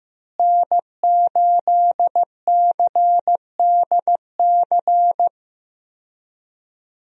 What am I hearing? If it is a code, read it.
Morse code, heard as a keyed tone, about 15 wpm, 700 Hz. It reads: N8CDC
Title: USA Call Signs - 5 Characters - Repeated 2x - 15wpm